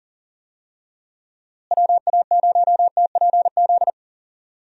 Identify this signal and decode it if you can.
Morse code, heard as a keyed tone, about 40 wpm, 700 Hz. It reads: WA0TPZ
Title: USA Call Signs - 6 Characters - Repeated 2x - 40wpm